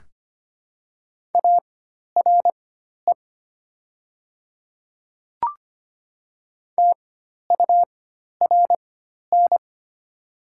Morse code, heard as a keyed tone, about 25 wpm, spaced as for 12 wpm, 700 Hz. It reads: ARE TURN
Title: 4x Character Spacing - Top 500 Words - 25wpm